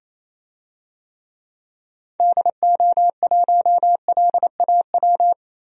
Morse code, heard as a keyed tone, about 28 wpm, 700 Hz. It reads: DO1LAW
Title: DXCC-100 Call Signs - 6 Characters - Repeated 2x - 28wpm